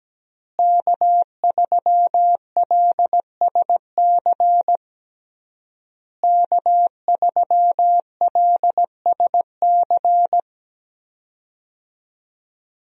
Morse code, heard as a keyed tone, about 17 wpm, 700 Hz. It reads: K3LSC K3LSC
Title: USA Call Signs - 5 Characters - Repeated 3x - 17wpm